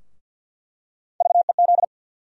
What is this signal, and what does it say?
Morse code, heard as a keyed tone, about 48 wpm, 700 Hz. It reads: VEZ
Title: USA Call Sign Postfixes - 3 Letters 50wpm